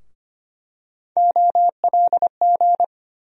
Morse code, heard as a keyed tone, about 25 wpm, 700 Hz. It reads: OLG
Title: USA Call Sign Postfixes - 3 Letters 25wpm